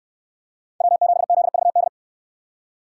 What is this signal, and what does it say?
Morse code, heard as a keyed tone, about 69 wpm, 700 Hz. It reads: W7ZVD